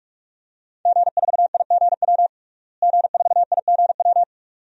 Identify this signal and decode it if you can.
Morse code, heard as a keyed tone, about 42 wpm, 700 Hz. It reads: G4IGW G4IGW
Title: DXCC-100 Call Signs - 5 Characters - Repeated 2x - 45wpm